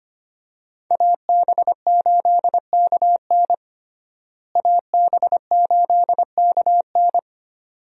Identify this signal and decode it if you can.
Morse code, heard as a keyed tone, about 25 wpm, 700 Hz. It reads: AB8KN AB8KN
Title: USA Call Signs - 5 Characters - Repeated 2x - 25wpm